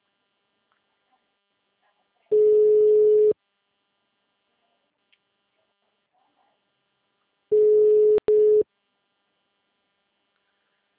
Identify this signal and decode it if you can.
Ringback tone (Europe, 425 Hz).
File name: out- -2022- - - .10